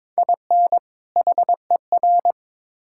Morse code, heard as a keyed tone, about 22 wpm, 700 Hz. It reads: IN HER
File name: Build A Phrase - 2-Word Phrases - 22wpm